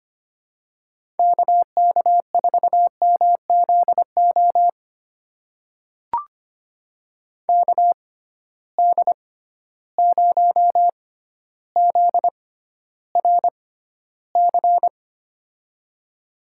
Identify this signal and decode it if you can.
Morse code, heard as a keyed tone, about 25 wpm, spaced as for 13 wpm, 700 Hz. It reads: KK4MZO KD0ZRC
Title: USA Call Signs - 6 Characters - 6x Character Spacing - 25wpm